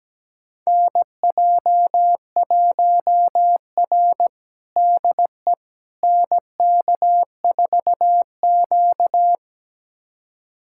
Morse code, heard as a keyed tone, about 17 wpm, 700 Hz. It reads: NJ1R DE NK4Q
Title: USA Callsign DE Callsign - 4 Characters - 17wpm